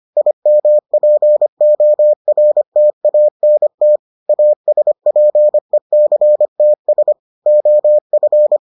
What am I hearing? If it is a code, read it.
Morse code, heard as a keyed tone, about 25 wpm, 600 Hz. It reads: IMPORTANT ASPECTS OF